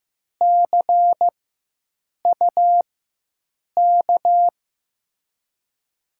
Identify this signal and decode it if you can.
Morse code, heard as a keyed tone, about 15 wpm, spaced as for 7 wpm, 700 Hz. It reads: CUK
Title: USA Call Signs - 6 Characters - 4x Character Spacing - 15wpm